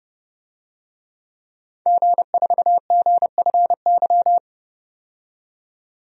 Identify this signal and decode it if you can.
Morse code, heard as a keyed tone, about 31 wpm, 700 Hz. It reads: G4GFY
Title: DXCC-100 Call Signs - 5 Characters - 30wpm